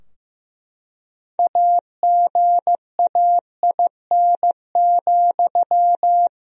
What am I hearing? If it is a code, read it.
Morse code, heard as a keyed tone, about 15 wpm, 700 Hz. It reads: AGAIN,
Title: Sentences from Top 1000 Words - Review Part 3 - 15wpm